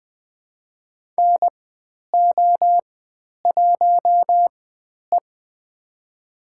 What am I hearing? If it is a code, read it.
Morse code, heard as a keyed tone, about 20 wpm, spaced as for 10 wpm, 700 Hz. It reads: NO1E
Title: 102 Lesson 17g - Letter u - Call Signs 20wpm